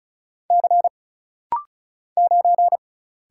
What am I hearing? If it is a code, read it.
Morse code, heard as a keyed tone, about 36 wpm, 700 Hz. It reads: C 9
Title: Single Letter-Number - Rapid-Fire 35wpm